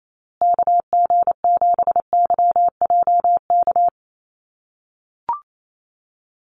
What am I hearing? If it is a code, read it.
Morse code, heard as a keyed tone, about 28 wpm, 700 Hz. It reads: KG7YJK
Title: USA Call Signs Repeated 3x 28wpm